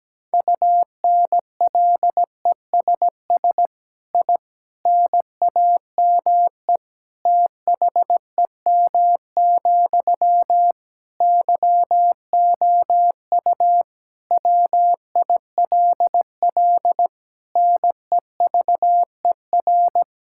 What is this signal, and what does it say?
Morse code, heard as a keyed tone, about 17 wpm, 700 Hz. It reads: UNLESS I NAME THEM, YOU WILL NEVER
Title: Sentences from Top 1000 Words - Review Part 6 - 17wpm